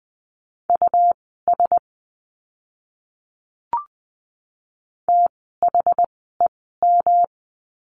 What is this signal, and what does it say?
Morse code, heard as a keyed tone, about 20 wpm, spaced as for 14 wpm, 700 Hz. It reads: US THEM
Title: 192 Lesson 31f - Letter v - Words 20wpm